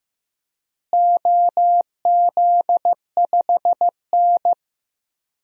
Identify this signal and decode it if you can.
Morse code, heard as a keyed tone, about 15 wpm, 700 Hz. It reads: OZ5N
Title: DXCC-100 Call Signs - 4 Characters - Repeated 3x - 15wpm